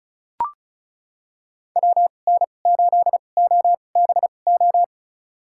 Morse code, heard as a keyed tone, about 36 wpm, spaced as for 25 wpm, 700 Hz. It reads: WN8OBO
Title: USA Call Signs - 6 Characters - 2x Character Spacing - 35wpm